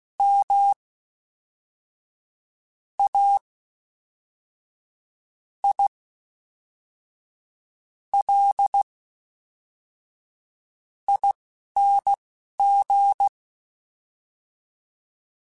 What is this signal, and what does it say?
Morse code, heard as a keyed tone, about 16 wpm, spaced as for 5 wpm, 800 Hz. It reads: MAILING
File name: Special INGs / Special INGs-mp3s / Fast ing @10